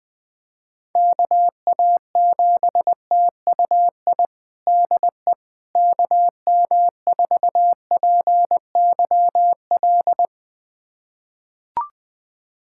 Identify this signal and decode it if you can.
Morse code, heard as a keyed tone, about 21 wpm, 700 Hz. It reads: KA7TUI DE KM4PYL
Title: USA Callsign DE Callsign 20wpm